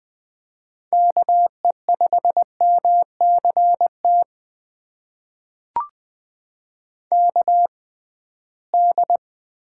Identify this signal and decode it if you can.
Morse code, heard as a keyed tone, about 20 wpm, spaced as for 15 wpm, 700 Hz. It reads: KE5MCT KD
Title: USA Call Signs - 6x Character Spacing - 20wpm